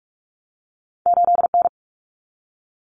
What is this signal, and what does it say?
Morse code, heard as a keyed tone, about 46 wpm, 700 Hz. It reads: 8N
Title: Common Cut Numbers - 2-Digit 45wpm